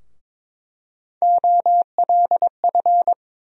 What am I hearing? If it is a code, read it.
Morse code, heard as a keyed tone, about 22 wpm, 700 Hz. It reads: OLF
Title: USA Call Sign Postfixes - 3 Letters 22wpm